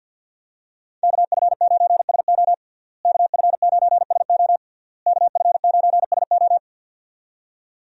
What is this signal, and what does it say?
Morse code, heard as a keyed tone, about 53 wpm, 700 Hz. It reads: KF9SO KF9SO KF9SO
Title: USA Call Signs - 5 Characters - Repeated 3x - 50wpm